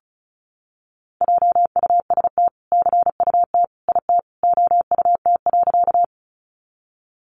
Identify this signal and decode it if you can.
Morse code, heard as a keyed tone, about 34 wpm, 700 Hz. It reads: JUST CUT IT OUT.
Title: Sentences from Top 300 Words 35wpm